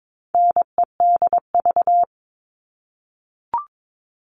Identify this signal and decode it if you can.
Morse code, heard as a keyed tone, about 22 wpm, 700 Hz. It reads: NEDV
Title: All Cut Numbers - 4-Digit 22wpm